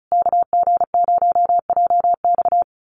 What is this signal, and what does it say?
Morse code, heard as a keyed tone, about 34 wpm, 700 Hz. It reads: KG0JX
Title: USA Call Signs Repeated 3x 35wpm